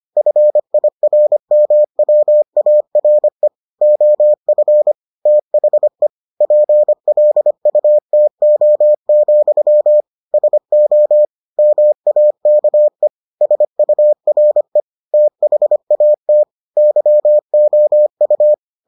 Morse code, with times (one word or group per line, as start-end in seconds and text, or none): 0.2-3.5 FIRMWARE
3.8-4.9 OF
5.3-6.1 THE
6.4-10.0 PLUTO,
10.3-11.2 SO
11.6-13.1 MAKE
13.4-14.8 SURE
15.1-16.4 THAT
16.8-18.5 YOU